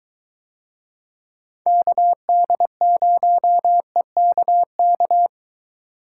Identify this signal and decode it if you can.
Morse code, heard as a keyed tone, about 22 wpm, 700 Hz. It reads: KD0EKK